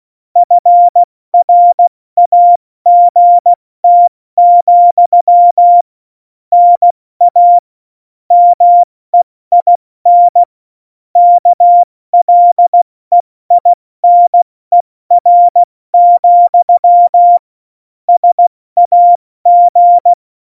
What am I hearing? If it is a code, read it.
Morse code, heard as a keyed tone, about 16 wpm, 700 Hz. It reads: FRAGT, NA MEIN KLEINER, SAG
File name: Morsekurs / Witze / Witz 29. Januar 2019, 70 F80